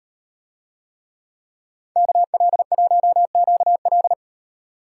Morse code, heard as a keyed tone, about 36 wpm, 700 Hz. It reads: KL1QL